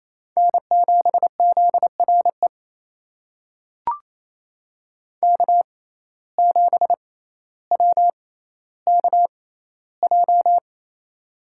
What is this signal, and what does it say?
Morse code, heard as a keyed tone, about 28 wpm, spaced as for 14 wpm, 700 Hz. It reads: N7ZRE K7WKJ
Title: USA Call Signs - 5 Characters - 6x Character Spacing - 28wpm